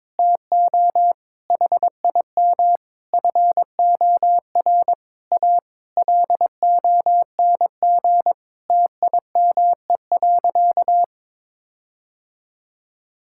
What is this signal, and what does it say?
Morse code, heard as a keyed tone, about 22 wpm, 700 Hz. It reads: TO HIM FOR A LONG TIME.
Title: Sentences from Top 1600 Words 22wpm